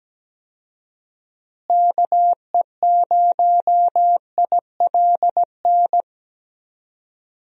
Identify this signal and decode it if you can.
Morse code, heard as a keyed tone, about 17 wpm, 700 Hz. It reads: KE0ILN
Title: USA Call Signs - 6 Characters - Repeated 2x - 17wpm